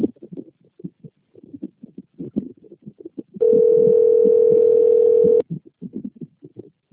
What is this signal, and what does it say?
Ringback tone (North America), 440 + 480 Hz.